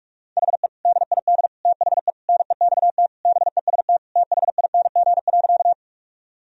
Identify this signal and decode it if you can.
Morse code, heard as a keyed tone, about 46 wpm, 700 Hz. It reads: HE DID THE NEXT BEST THING.
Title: Sentences from Top 1000 Words - Review Part 1 - 45wpm